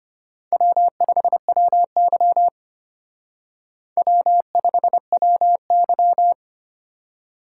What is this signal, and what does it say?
Morse code, heard as a keyed tone, about 30 wpm, 700 Hz. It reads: W5WY W5WY
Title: Speed Racing - USA 1x2 Call Signs 25wpm